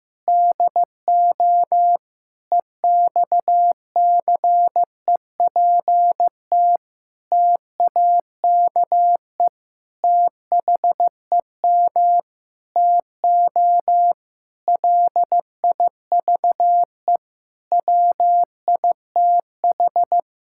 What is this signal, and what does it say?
Morse code, heard as a keyed tone, about 15 wpm, 700 Hz. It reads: DO EXCEPT TAKE THEM TO LIVE WITH
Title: Sentences from Top 1600 Words 15wpm